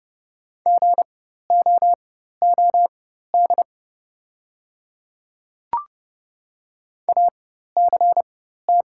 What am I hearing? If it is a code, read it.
Morse code, heard as a keyed tone, about 30 wpm, spaced as for 14 wpm, 700 Hz. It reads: GOOD ACT
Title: 4x Character Spacing - Top 200 Words - 30wpm